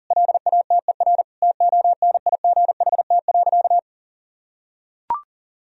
Morse code, heard as a keyed tone, about 40 wpm, 700 Hz. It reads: LATER TONIGHT.
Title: Sentences from Top 1000 Words - Review Part 4 - 40wpm